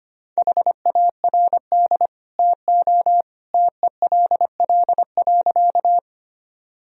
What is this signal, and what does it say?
Morse code, heard as a keyed tone, about 25 wpm, 700 Hz. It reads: HARD TO TELL.
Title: Sentences from Top 1000 Words - Review Part 1 - 25wpm